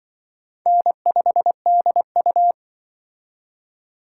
Morse code, heard as a keyed tone, about 24 wpm, 700 Hz. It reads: N5DU
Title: Speed Racing - USA 1x2 Call Signs 20wpm